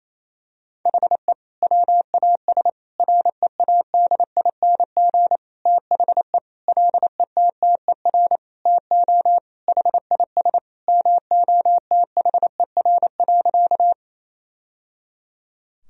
Morse code, heard as a keyed tone, about 27 wpm, 700 Hz. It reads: HE WAS READING THE LETTER TO HIS MOTHER.
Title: Sentences from Top 1000 Words - Review Part 3 - 28wpm